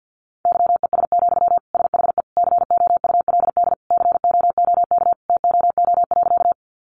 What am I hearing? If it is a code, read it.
Morse code, heard as a keyed tone, about 50 wpm, 700 Hz. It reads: YES, SHE COULD COOK TOO.
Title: Sentences from Top 700 Words 50wpm